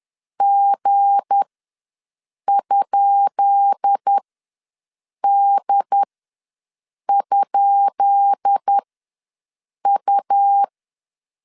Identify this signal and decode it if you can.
Morse code, heard as a keyed tone, about 10 wpm, spaced as for 6 wpm, 800 Hz. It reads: G?D?U
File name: LEZIONI / 10 SIGNS / 10B INT ws 10wpm 4134